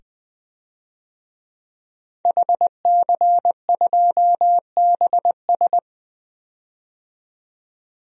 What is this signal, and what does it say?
Morse code, heard as a keyed tone, about 20 wpm, 700 Hz. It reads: HC2BS
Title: DXCC-100 Call Signs - 5 Characters - 20wpm